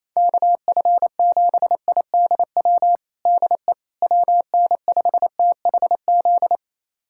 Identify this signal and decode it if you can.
Morse code, heard as a keyed tone, about 28 wpm, 700 Hz. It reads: KF7IDW DE WN5THZ
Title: USA Callsign DE Callsign - 6 Characters - 28wpm